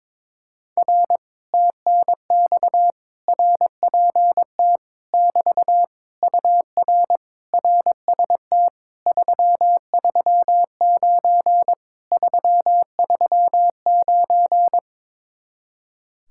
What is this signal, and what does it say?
Morse code, heard as a keyed tone, about 22 wpm, 700 Hz. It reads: R TNX RPT <BT> UR RST 339 339
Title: Speed-Racing Common QSO Phrases - 22wpm